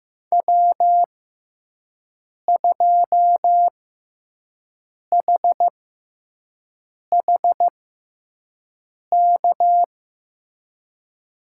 Morse code, heard as a keyed tone, about 15 wpm, spaced as for 5 wpm, 700 Hz. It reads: W2HHK
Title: USA Call Signs - 5 Characters - 6x Character Spacing - 15wpm